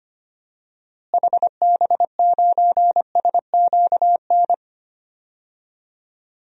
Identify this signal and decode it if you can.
Morse code, heard as a keyed tone, about 25 wpm, 700 Hz. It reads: HB9SQN
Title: DXCC-100 Call Signs - 6 Characters - Repeated 2x - 25wpm